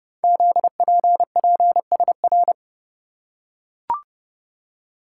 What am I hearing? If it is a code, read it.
Morse code, heard as a keyed tone, about 28 wpm, 690 Hz. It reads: ZPPSR
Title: Code Groups - 5 Letters - 30wpm